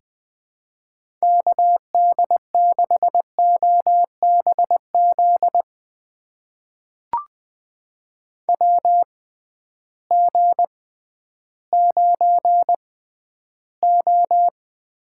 Morse code, heard as a keyed tone, about 20 wpm, spaced as for 12 wpm, 700 Hz. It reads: KD6OBZ WG9O
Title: USA Call Signs - 6x Character Spacing - 20wpm